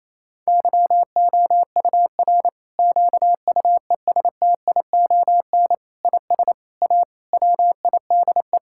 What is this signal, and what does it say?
Morse code, heard as a keyed tone, about 28 wpm, 700 Hz. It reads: YOUR QUESTION IS A WIDE
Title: Sentences from Top 1000 Words - Review Part 6 - 28wpm